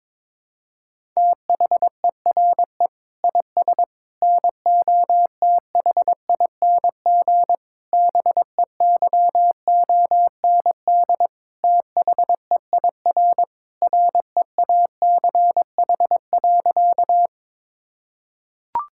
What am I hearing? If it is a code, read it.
Morse code, heard as a keyed tone, about 22 wpm, 700 Hz. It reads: THERE IS NOTHING BEYOND THEIR REACH.
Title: Sentences from Top 1000 Words - Review Part 6 - 22wpm